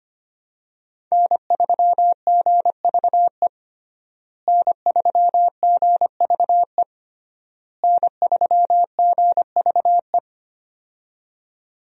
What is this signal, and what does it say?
Morse code, heard as a keyed tone, about 25 wpm, 700 Hz. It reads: N3GVE N3GVE N3GVE
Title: USA Call Signs - 5 Characters - Repeated 3x - 25wpm